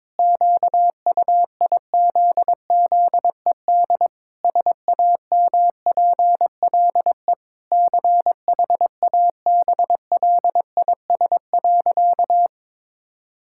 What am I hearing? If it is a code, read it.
Morse code, heard as a keyed tone, about 22 wpm, 700 Hz. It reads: QUIZZED SAMPLE CHABLIS.
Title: Pangram Challenge v2 - 22wpm